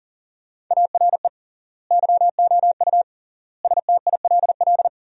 Morse code, heard as a keyed tone, about 40 wpm, 700 Hz. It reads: ARE YOU STILL
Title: 3x Word-Spacing - Sentences from Top 400 Words - 40wpm